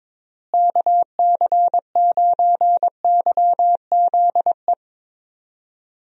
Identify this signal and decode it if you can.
Morse code, heard as a keyed tone, about 22 wpm, 700 Hz. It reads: KC9YZE
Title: USA Call Signs - 6 Characters - Repeated 3x - 22wpm